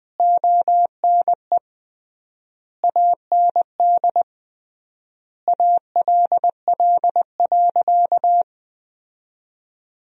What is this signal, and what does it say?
Morse code, heard as a keyed tone, about 20 wpm, 700 Hz. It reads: ONE AND ALL.
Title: 3x Word-Spacing - Sentences from Top 200 Words - 20wpm